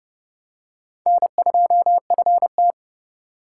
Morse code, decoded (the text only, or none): N2FT